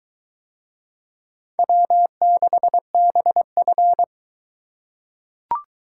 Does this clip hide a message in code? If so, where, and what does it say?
Yes: Morse code from 1.6 to 4.0 s, heard as a keyed tone, about 22 wpm, 700 Hz. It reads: W6BF